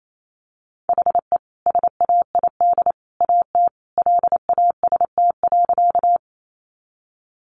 Morse code, heard as a keyed tone, about 28 wpm, 700 Hz. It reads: HE SAID AT LAST.
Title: Sentences from Top 300 Words 28wpm